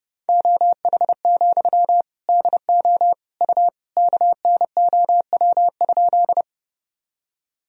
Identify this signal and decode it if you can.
Morse code, heard as a keyed tone, about 30 wpm, 700 Hz. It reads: OH, DO U KNOW?
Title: Sentences from Top 500 Words and Abbreviations 30wpm